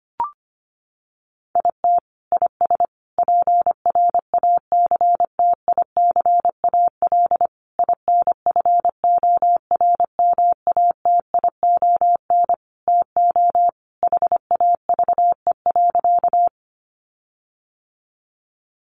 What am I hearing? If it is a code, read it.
Morse code, heard as a keyed tone, about 25 wpm, 700 Hz. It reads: IT IS PRACTICAL INFORMATION TO HAVE.
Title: Sentences from Top 1400 Words 25wpm